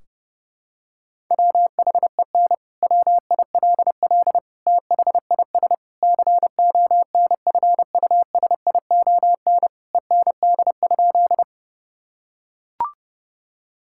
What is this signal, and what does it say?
Morse code, heard as a keyed tone, about 32 wpm, 700 Hz. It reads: WHEN WILL THIS CONFUSION END?
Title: Sentences from Top 1600 Words 30wpm